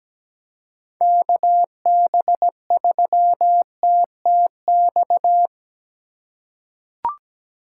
Morse code, heard as a keyed tone, about 17 wpm, 700 Hz. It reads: KB3TTX
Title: USA Call Signs - 6 Characters - 4x Character Spacing - 17wpm